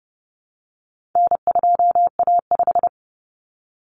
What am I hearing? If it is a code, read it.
Morse code, heard as a keyed tone, about 30 wpm, 700 Hz. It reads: N2A5
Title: Common Cut Numbers - 4-Digit 30wpm